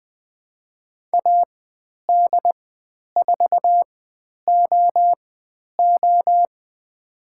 Morse code, heard as a keyed tone, about 20 wpm, spaced as for 10 wpm, 700 Hz. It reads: AD4OO
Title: 109 Lesson 18g - Letter m - Call Signs 20wpm